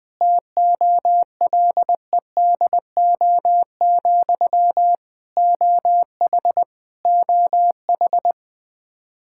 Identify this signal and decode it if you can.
Morse code, heard as a keyed tone, about 20 wpm, 700 Hz. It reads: TOLEDO, OH OH
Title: Practice QSO - Exchange 1 - Beginning - 20wpm